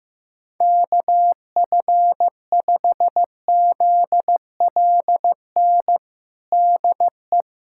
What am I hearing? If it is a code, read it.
Morse code, heard as a keyed tone, about 16 wpm, 700 Hz. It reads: KF5ZLN DE
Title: USA Callsign DE Callsign 15wpm